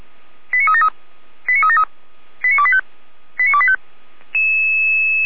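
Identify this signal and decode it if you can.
Five-tone selective call (ZVEI), heard as quick runs of five tones, tones of about 70 ms. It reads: 89373 89373 89285 89285
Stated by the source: ZVEI five-tone selective call received on SDR.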